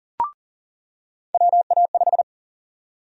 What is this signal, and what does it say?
Morse code, heard as a keyed tone, about 40 wpm, 700 Hz. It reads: WA5
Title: USA Call Sign Prefixes - 40wpm